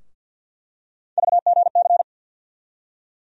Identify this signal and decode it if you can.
Morse code, heard as a keyed tone, about 52 wpm, 700 Hz. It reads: VGC
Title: USA Call Sign Postfixes - 3 Letters 50wpm